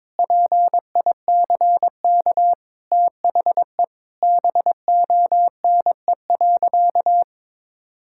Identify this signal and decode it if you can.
Morse code, heard as a keyed tone, about 22 wpm, 700 Hz. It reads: PICK THE BONE.
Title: Sentences from Top 1600 Words 22wpm